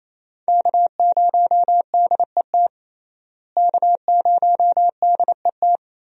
Morse code, heard as a keyed tone, about 27 wpm, 700 Hz. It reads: K0DET K0DET
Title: USA Call Signs - 5 Characters - Repeated 3x - 28wpm